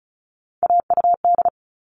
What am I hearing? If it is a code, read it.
Morse code, heard as a keyed tone, about 34 wpm, 700 Hz. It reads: AUD
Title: All Cut Numbers - 3-Digit 35wpm